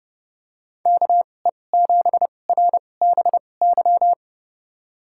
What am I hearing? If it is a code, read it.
Morse code, heard as a keyed tone, about 29 wpm, spaced as for 22 wpm, 700 Hz. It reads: KE7RBY
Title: USA Call Signs - 6 Characters - 2x Character Spacing - 30wpm